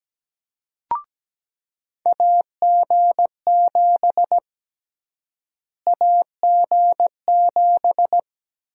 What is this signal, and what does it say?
Morse code, heard as a keyed tone, about 17 wpm, 700 Hz. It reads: AG7 AG7
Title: USA Call Sign Prefixes - Repeated 2x - 17wpm